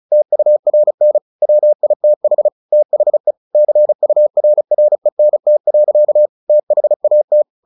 Morse code, heard as a keyed tone, about 34 wpm, 600 Hz. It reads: TURN WITH THE CURRENT. THAT
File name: the old man and the sea / the old man and the sea 1 35wpm